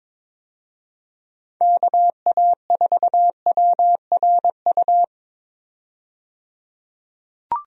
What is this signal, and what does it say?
Morse code, heard as a keyed tone, about 22 wpm, 700 Hz. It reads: KA4WRU
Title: USA Call Signs - 6 Characters - 22wpm